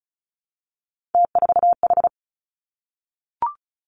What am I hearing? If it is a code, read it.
Morse code, heard as a keyed tone, about 36 wpm, 700 Hz. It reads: T4H
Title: Three Challenging Letter-Number Combinations 35wpm